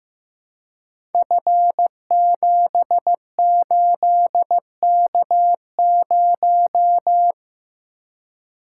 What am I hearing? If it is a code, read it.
Morse code, heard as a keyed tone, about 15 wpm, 700 Hz. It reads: F78K0
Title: Code Groups - 5 Letter:Numbers - 15wpm